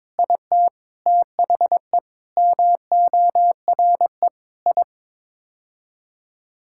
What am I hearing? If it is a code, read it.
Morse code, heard as a keyed tone, about 21 wpm, 700 Hz. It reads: IT THE MORE I
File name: Sets of 9 Words - 22wpm